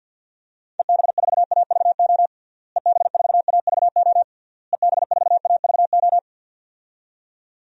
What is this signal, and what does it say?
Morse code, heard as a keyed tone, about 47 wpm, 700 Hz. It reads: EB4AVO EB4AVO EB4AVO
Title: DXCC-100 Call Signs - 6 Characters - Repeated 3x - 50wpm